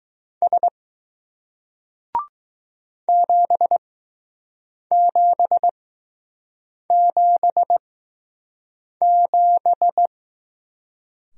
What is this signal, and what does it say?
Morse code, heard as a keyed tone, about 20 wpm, spaced as for 6 wpm, 700 Hz. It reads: S 7777